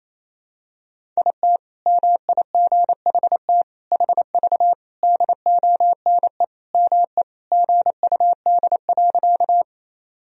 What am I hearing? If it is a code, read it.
Morse code, heard as a keyed tone, about 28 wpm, 700 Hz. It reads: IT MIGHT HV DONE ME GUD.
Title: Sentences from Top 500 Words and Abbreviations 28wpm